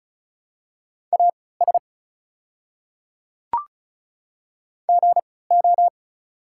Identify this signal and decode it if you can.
Morse code, heard as a keyed tone, about 36 wpm, spaced as for 20 wpm, 690 Hz. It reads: AS GO
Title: 3x Character Spacing - Top 100 Words - 35wpm